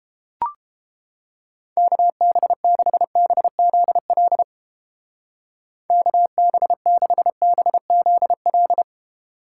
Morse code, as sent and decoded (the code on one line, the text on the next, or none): -.- -... -.... -... --.. .-.. / -.- -... -.... -... --.. .-..
KB6BZL KB6BZL